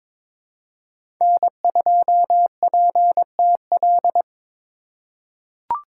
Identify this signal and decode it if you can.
Morse code, heard as a keyed tone, about 21 wpm, 700 Hz. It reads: N2PTL
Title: USA Call Signs - 5 Characters - 4x Character Spacing - 22wpm